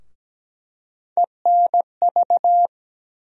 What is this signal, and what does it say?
Morse code, heard as a keyed tone, about 17 wpm, 700 Hz. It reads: ENV
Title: USA Call Sign Postfixes - 3 Letters 17wpm